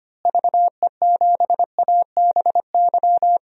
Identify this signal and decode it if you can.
Morse code, heard as a keyed tone, about 25 wpm, 700 Hz. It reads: VE7ABY
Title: DXCC-100 Call Signs - 6 Characters - Repeated 2x - 25wpm